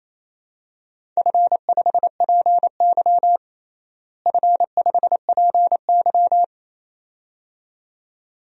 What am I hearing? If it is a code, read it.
Morse code, heard as a keyed tone, about 28 wpm, 700 Hz. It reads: F5PY F5PY
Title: DXCC-100 Call Signs - 4 Characters - Repeated 2x - 28wpm